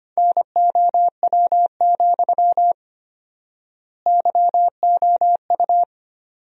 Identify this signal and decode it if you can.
Morse code, heard as a keyed tone, about 25 wpm, 700 Hz. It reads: NOW, YOU
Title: 4x Word-Spacing - Sentences from Top 200 Words - 25wpm